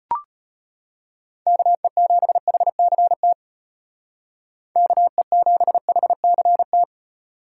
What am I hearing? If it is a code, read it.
Morse code, heard as a keyed tone, about 30 wpm, 700 Hz. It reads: KE7HCT KE7HCT
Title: Speed Racing - USA 2x3 Call Signs 25wpm